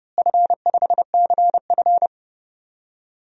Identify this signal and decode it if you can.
Morse code, heard as a keyed tone, about 30 wpm, 700 Hz. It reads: F5CF
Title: DXCC-100 Call Signs - 4 Characters - Repeated 3x - 30wpm